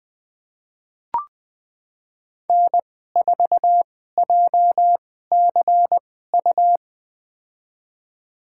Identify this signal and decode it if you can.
Morse code, heard as a keyed tone, about 20 wpm, spaced as for 14 wpm, 700 Hz. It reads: N4JCU
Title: USA Call Signs - 5 Characters - 2x Character Spacing - 20wpm